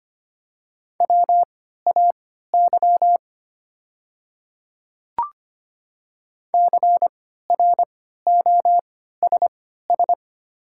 Morse code, heard as a keyed tone, about 25 wpm, spaced as for 14 wpm, 700 Hz. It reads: WAY CROSS
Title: 3x Character Spacing - Top 300 Words - 25wpm